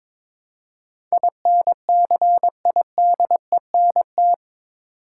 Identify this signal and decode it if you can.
Morse code, heard as a keyed tone, about 22 wpm, 700 Hz. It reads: INCIDENT
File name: practice / 22wpm / Eight Letter Words - 22wpm